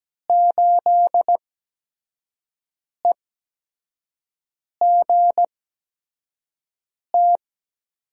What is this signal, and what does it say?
Morse code, heard as a keyed tone, about 17 wpm, spaced as for 5 wpm, 700 Hz. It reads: 8EGT
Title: USA Call Signs - 8x Character Spacing - 17wpm